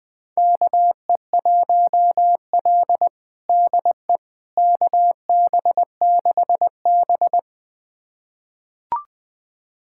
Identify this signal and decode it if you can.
Morse code, heard as a keyed tone, about 20 wpm, 700 Hz. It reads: KE1L DE KB6B
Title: USA Callsign DE Callsign - 4 Characters - 20wpm